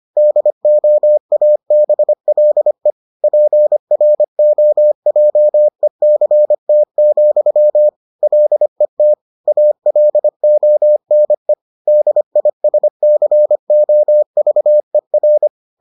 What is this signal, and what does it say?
Morse code, heard as a keyed tone, about 25 wpm, 600 Hz. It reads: DOABLE PROJECT, LET ALONE DISCOVER